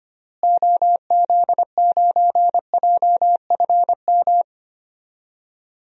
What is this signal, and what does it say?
Morse code, heard as a keyed tone, about 25 wpm, 700 Hz. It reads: OZ9JFM
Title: DXCC-100 Call Signs - 6 Characters - Repeated 2x - 25wpm